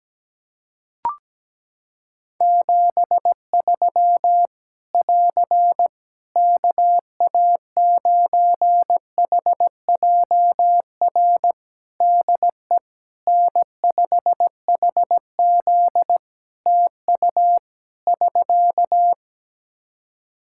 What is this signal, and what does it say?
Morse code, heard as a keyed tone, about 17 wpm, 700 Hz. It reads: 73 <AR> KA9HJR DE N5HZ TU <SK>
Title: Practice QSO - Exchange 4 - Ending - 17wpm